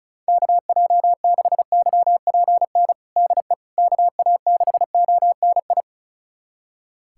Morse code, heard as a keyed tone, about 36 wpm, 700 Hz. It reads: KJ6YPN DE KA6ONI
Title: USA Callsign DE Callsign - 6 Characters - 35wpm